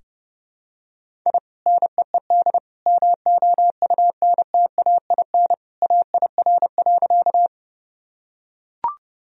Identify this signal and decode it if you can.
Morse code, heard as a keyed tone, about 30 wpm, 700 Hz. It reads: I NEED MOUNTAIN AIR.
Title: Sentences from Top 1000 Words - Review Part 1 - 30wpm